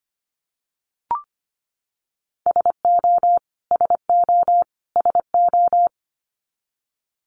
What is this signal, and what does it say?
Morse code, heard as a keyed tone, about 25 wpm, 700 Hz. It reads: SO SO SO
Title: Top 500 Words Repeated 25wpm